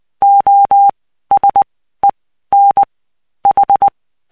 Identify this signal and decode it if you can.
Morse code, heard as a keyed tone, about 20 wpm, spaced as for 13 wpm, 800 Hz. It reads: OSEN H
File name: cw 75BPM